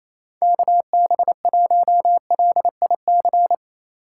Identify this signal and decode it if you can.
Morse code, heard as a keyed tone, about 34 wpm, 700 Hz. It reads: KB1LIC
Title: Speed Racing - USA 2x3 Call Signs 28wpm